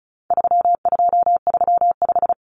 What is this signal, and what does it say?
Morse code, heard as a keyed tone, about 34 wpm, 700 Hz. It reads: 3235
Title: Numbers - 4-Digit 35wpm